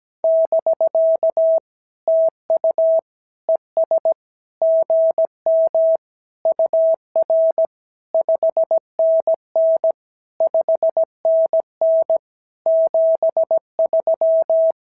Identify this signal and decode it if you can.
Morse code, heard as a keyed tone, about 16 wpm, 650 Hz. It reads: <BK> TU ES GM UR 5NN 5NN 73
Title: SOTA QSO Practice 17wpm